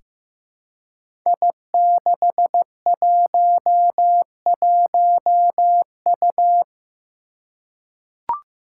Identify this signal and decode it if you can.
Morse code, heard as a keyed tone, about 15 wpm, 700 Hz. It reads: I611U
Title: Five Challenging Letter-Number Combinations 15wpm